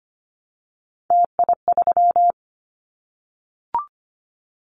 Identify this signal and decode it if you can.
Morse code, heard as a keyed tone, about 25 wpm, 700 Hz. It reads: TI3
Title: Callsign Prefixes for Top 100 DXCC - 25wpm